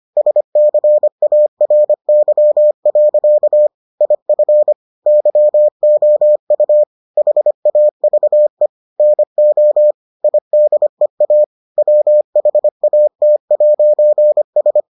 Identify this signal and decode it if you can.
Morse code, heard as a keyed tone, about 25 wpm, 600 Hz. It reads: SCARY. IF YOU HAVE NO IDEA WHAT'S